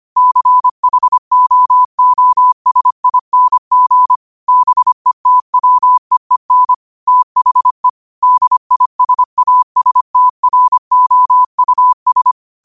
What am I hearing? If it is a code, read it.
Morse code, heard as a keyed tone, about 25 wpm, 1000 Hz. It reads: CHOOSING BETWEEN THE DISASTROUS
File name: qotdfiles / 25wpm